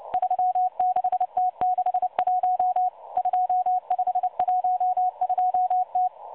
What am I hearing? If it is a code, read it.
Morse code, heard as a keyed tone, about 29 wpm, 730 Hz. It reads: 36T61 2512T
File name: M12 - 8116 kHz CW 2-11-14